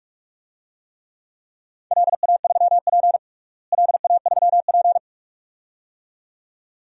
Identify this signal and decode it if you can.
Morse code, heard as a keyed tone, about 45 wpm, 700 Hz. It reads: LA3P LA3P
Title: DXCC-100 Call Signs - 4 Characters - Repeated 2x - 45wpm